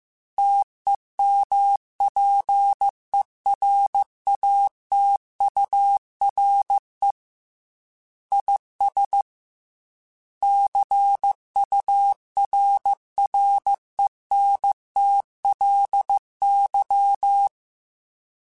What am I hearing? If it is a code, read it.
Morse code, heard as a keyed tone, about 15 wpm, 800 Hz. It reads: TEMPERATURE IS CURRENTLY
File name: Weather Sets / Weather Sets-mp3s / Weather 5 1 @13